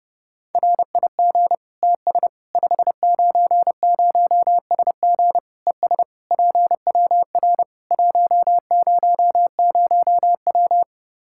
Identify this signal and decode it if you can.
Morse code, heard as a keyed tone, about 32 wpm, 700 Hz. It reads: RIG TS 590SG ES PWR 100W
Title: Practice QSO - Exchange 2 - Info - 30wpm